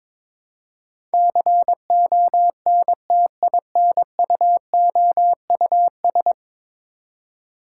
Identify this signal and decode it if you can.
Morse code, heard as a keyed tone, about 21 wpm, 700 Hz. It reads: CONTINUOUS
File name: practice / 22wpm / 10 Letter Words - First 3 Letters Given - 22wpm